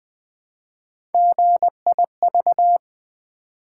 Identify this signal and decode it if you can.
Morse code, heard as a keyed tone, about 20 wpm, 700 Hz. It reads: GIV
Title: Build A Word - 1 Letter At A Time - 4-Letter Words - 20wpm